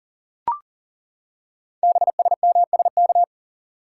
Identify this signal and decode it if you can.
Morse code, heard as a keyed tone, about 42 wpm, 700 Hz. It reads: BSMSK
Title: Code Groups - 5 Letters - 40wpm